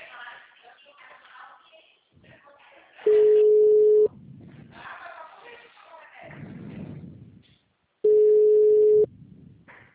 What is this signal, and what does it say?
Ringback tone (Europe, 425 Hz), with speech.
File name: out- -2006- - - .40645